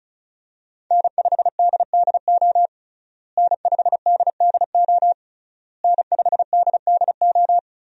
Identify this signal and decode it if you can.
Morse code, heard as a keyed tone, about 34 wpm, 700 Hz. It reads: N5DDO N5DDO N5DDO
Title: USA Call Signs - 5 Characters - Repeated 3x - 35wpm